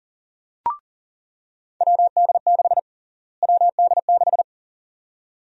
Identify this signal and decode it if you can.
Morse code, heard as a keyed tone, about 38 wpm, 700 Hz. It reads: WD6 WD6
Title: USA Call Sign Prefixes - Repeated 2x - 40wpm